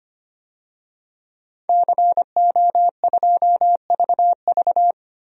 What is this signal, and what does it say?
Morse code, heard as a keyed tone, about 25 wpm, 700 Hz. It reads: CO2VV
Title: DXCC-100 Call Signs - 5 Characters - 25wpm